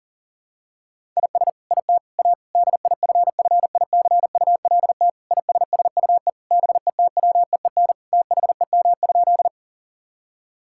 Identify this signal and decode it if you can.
Morse code, heard as a keyed tone, about 37 wpm, 700 Hz. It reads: IS IT A DIFFICULT ISSUE BETWEEN THEM?
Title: Sentences from Top 1000 Words - Review Part 6 - 40wpm